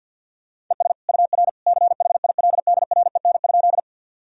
Morse code, heard as a keyed tone, about 47 wpm, 700 Hz. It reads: ES UR CHILDREN?
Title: Sentences from Top 300 Words and Abbreviations 50wpm